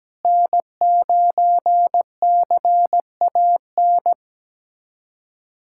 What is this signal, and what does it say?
Morse code, heard as a keyed tone, about 17 wpm, 700 Hz. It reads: N9CAN
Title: USA Callsign DE Callsign - 5 Characters - 17wpm